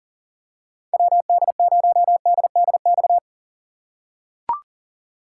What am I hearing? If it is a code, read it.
Morse code, heard as a keyed tone, about 40 wpm, 700 Hz. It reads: WD0DDX
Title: USA Call Signs - 6 Characters - 3x Character Spacing - 40wpm